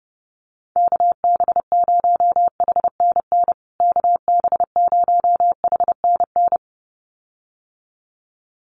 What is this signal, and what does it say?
Morse code, heard as a keyed tone, about 30 wpm, 700 Hz. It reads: KB0HNN KB0HNN
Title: USA Call Signs Repeated 2x 30wpm